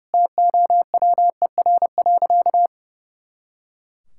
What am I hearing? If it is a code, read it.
Morse code, heard as a keyed tone, about 32 wpm, 700 Hz. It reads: TOWER.
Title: Pangram Challenge v2 - 30wpm